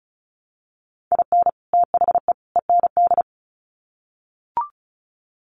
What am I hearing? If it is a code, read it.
Morse code, heard as a keyed tone, about 34 wpm, 700 Hz. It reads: IN THE END
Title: Sets of 3 Words 35wpm